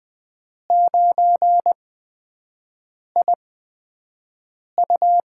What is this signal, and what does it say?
Morse code, heard as a keyed tone, about 20 wpm, spaced as for 5 wpm, 700 Hz. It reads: 9IU
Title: USA Call Signs - 4 Characters - 8x Character Spacing - 20wpm